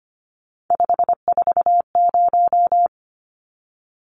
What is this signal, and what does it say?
Morse code, heard as a keyed tone, about 25 wpm, 700 Hz. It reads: 540
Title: Numbers - 3-Digit 25wpm